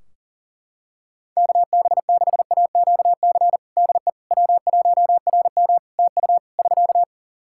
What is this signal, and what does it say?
Morse code, heard as a keyed tone, about 40 wpm, 700 Hz. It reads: KB6AQC DE W1RM TU <SK>
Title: Practice QSO - Exchange 4 - Ending - 40wpm